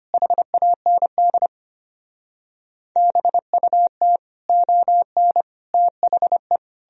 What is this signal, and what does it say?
Morse code, heard as a keyed tone, about 33 wpm, 700 Hz. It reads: HAND BUT ON THE